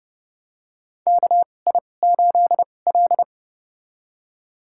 Morse code, heard as a keyed tone, about 30 wpm, spaced as for 22 wpm, 700 Hz. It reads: KI8L
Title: USA Call Signs - 4 Characters - 2x Character Spacing - 30wpm